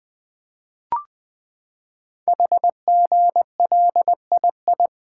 Morse code, heard as a keyed tone, about 20 wpm, 700 Hz. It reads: HGLII
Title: Code Groups - 5 Letters - 20wpm